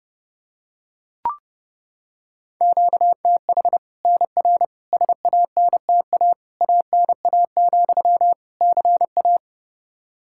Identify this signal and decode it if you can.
Morse code, heard as a keyed tone, about 30 wpm, 700 Hz. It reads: QTH NR SANTA ANA, CA
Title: Practice QSO - Exchange 1 - Beginning - 30wpm